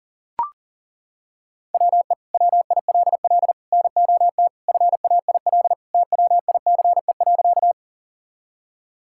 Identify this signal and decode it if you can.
Morse code, heard as a keyed tone, about 40 wpm, 700 Hz. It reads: WE WILL NOT FAIL TWICE.
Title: Sentences from Top 1600 Words 40wpm